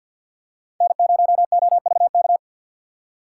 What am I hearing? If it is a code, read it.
Morse code, heard as a keyed tone, about 47 wpm, 700 Hz. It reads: N0OVK
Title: USA Call Signs - 5 Characters - 2x Character Spacing - 50wpm